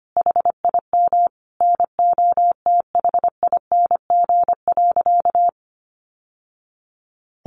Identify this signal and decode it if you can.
Morse code, heard as a keyed tone, about 25 wpm, 700 Hz. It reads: HIM NOTHING.
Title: Sentences from Top 700 Words 25wpm